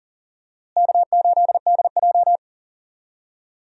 Morse code, heard as a keyed tone, about 40 wpm, 700 Hz. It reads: K8DJ
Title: USA Call Signs - 4 Characters - Repeated 2x - 40wpm